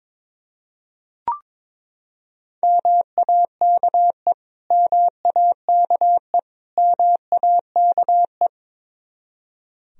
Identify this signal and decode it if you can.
Morse code, heard as a keyed tone, about 22 wpm, 700 Hz. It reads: MAKE MAKE MAKE
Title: Top 100 Words Repeated 22wpm